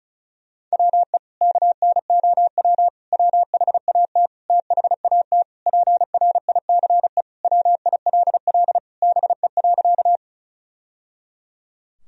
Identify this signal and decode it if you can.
Morse code, heard as a keyed tone, about 36 wpm, 700 Hz. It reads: WE KNOW WHAT THAT PRICE WILL BE.
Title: Sentences from Top 1000 Words - Review Part 4 - 35wpm